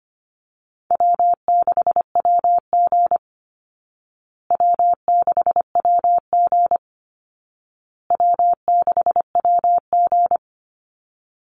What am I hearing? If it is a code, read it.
Morse code, heard as a keyed tone, about 25 wpm, 700 Hz. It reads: W6WG W6WG W6WG
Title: Call Signs Repeated 2x - 25wpm